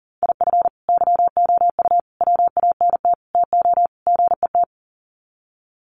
Morse code, heard as a keyed tone, about 40 wpm, 700 Hz. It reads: IF YOU WANT TO GET
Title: Sets of 5 Words Encore 40wpm